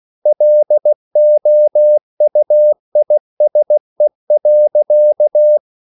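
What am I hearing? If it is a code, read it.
Morse code, heard as a keyed tone, about 16 wpm, 600 Hz. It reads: LOUISE.
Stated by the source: Slow (QRS) Morse code at 16 wpm, from CW med Gnister.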